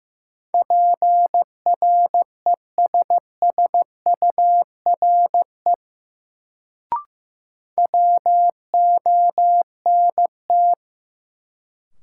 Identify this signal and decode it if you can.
Morse code, heard as a keyed tone, about 15 wpm, 700 Hz. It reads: PRESSURE WONT
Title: Sentences from Top 1000 Words - Review Part 6 - New Word Drill - 15wpm